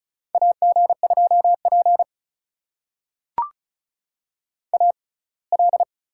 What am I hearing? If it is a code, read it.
Morse code, heard as a keyed tone, about 36 wpm, spaced as for 24 wpm, 700 Hz. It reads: AG2P AL
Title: USA Call Signs - 4 Characters - 6x Character Spacing - 35wpm